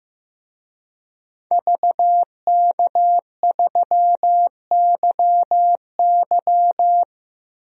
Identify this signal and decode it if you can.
Morse code, heard as a keyed tone, about 15 wpm, 700 Hz. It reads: VK3YY
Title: DXCC-100 Call Signs - 5 Characters - 15wpm